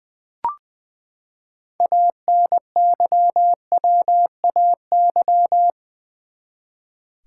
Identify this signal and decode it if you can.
Morse code, heard as a keyed tone, about 20 wpm, 700 Hz. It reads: ANYWAY
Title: Six Letter Words - 20wpm